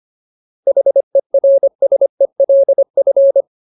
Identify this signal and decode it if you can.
Morse code, heard as a keyed tone, about 25 wpm, 550 Hz. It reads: HERSELF
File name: Ultimate Word List / 25WPM Ultimate